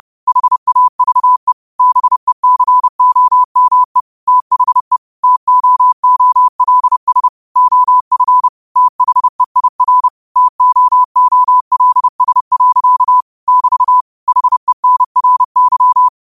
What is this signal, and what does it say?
Morse code, heard as a keyed tone, about 30 wpm, 1000 Hz. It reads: HAVE BECOME THE TOOLS OF THEIR TOOLS. <BT> HENRY
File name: qotdfiles / 30wpm